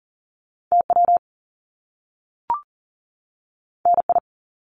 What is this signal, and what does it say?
Morse code, heard as a keyed tone, about 41 wpm, 700 Hz. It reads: TW NI
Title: Top Two Letter Combinations in Words 40wpm